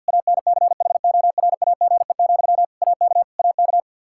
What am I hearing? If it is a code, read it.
Morse code, heard as a keyed tone, about 52 wpm, 700 Hz. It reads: ANCHORAGE, AK AK
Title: Practice QSO - Exchange 1 - Beginning - 50wpm